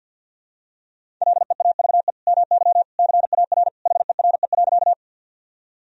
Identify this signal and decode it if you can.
Morse code, heard as a keyed tone, about 46 wpm, 700 Hz. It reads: LEAVE MY CAR HERE.
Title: Sentences from Top 1000 Words - Review Part 1 - 50wpm